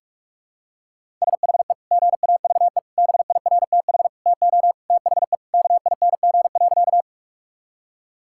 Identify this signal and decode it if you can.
Morse code, heard as a keyed tone, about 46 wpm, 700 Hz. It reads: SHE GAVE BIRTH TO THE KING.
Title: Sentences from Top 1000 Words - Review Part 5 - 45wpm